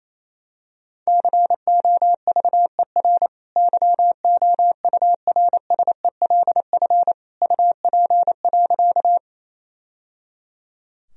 Morse code, heard as a keyed tone, about 28 wpm, 700 Hz. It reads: COVER YOURSELF UP.
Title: Sentences from Top 1000 Words - Review Part 1 - 28wpm